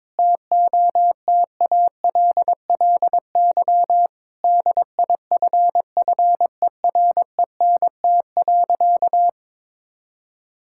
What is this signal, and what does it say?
Morse code, heard as a keyed tone, about 22 wpm, 700 Hz. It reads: TOTALLY DIFFERENT.
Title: Sentences from Top 1600 Words 22wpm